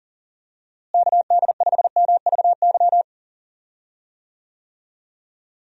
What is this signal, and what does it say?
Morse code, heard as a keyed tone, about 37 wpm, 700 Hz. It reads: KD5MVY
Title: USA Call Signs - 6 Characters - Repeated 2x - 40wpm